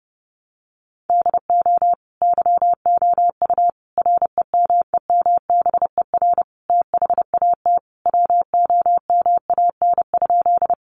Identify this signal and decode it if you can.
Morse code, heard as a keyed tone, about 30 wpm, 700 Hz. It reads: DO YOU REMEMBER THAT WOMAN?
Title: Sentences from Top 1100 Words 30wpm